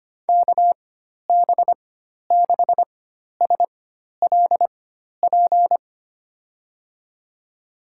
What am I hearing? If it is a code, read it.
Morse code, heard as a keyed tone, about 25 wpm, spaced as for 12 wpm, 700 Hz. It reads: KB6SLP
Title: USA Call Signs - 6 Characters - 4x Character Spacing - 25wpm